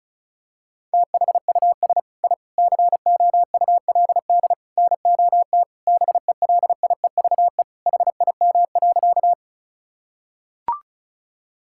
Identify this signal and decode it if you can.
Morse code, heard as a keyed tone, about 36 wpm, 700 Hz. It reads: THUS I COULD NOT BELIEVE HIM.
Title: Sentences from Top 1000 Words - Review Part 5 - 35wpm